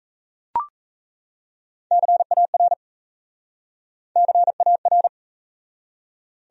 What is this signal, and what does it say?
Morse code, heard as a keyed tone, about 37 wpm, 700 Hz. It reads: CAR CAR